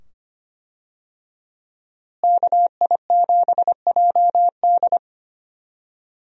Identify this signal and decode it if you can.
Morse code, heard as a keyed tone, about 25 wpm, 700 Hz. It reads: KI7JD
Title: USA Call Signs - 5 Characters - Repeated 2x - 25wpm